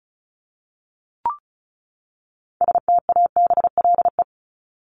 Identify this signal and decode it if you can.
Morse code, heard as a keyed tone, about 34 wpm, 700 Hz. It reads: STABLE
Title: Top 1251-1500 Words 35wpm